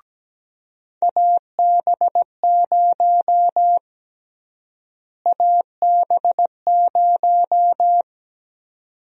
Morse code, heard as a keyed tone, about 17 wpm, 700 Hz. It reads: AB0 AB0
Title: USA Call Sign Prefixes - Repeated 2x - 17wpm